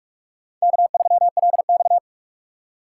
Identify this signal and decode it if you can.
Morse code, heard as a keyed tone, about 47 wpm, 700 Hz. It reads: K3LX
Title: USA Call Signs - 4 Characters - 2x Character Spacing - 45wpm